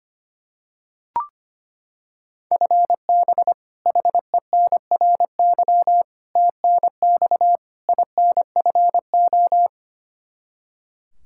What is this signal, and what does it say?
Morse code, heard as a keyed tone, about 25 wpm, 700 Hz. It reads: FB HENRY TNX INFO
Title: Practice QSO - Exchange 3 - More Info - 25wpm